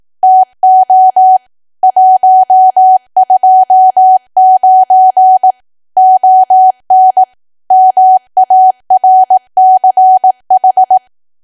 Morse code, heard as a keyed tone, about 18 wpm, 750 Hz. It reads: TO 129 ON MARCH
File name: cwa / W1AW-Level2 / 18WPM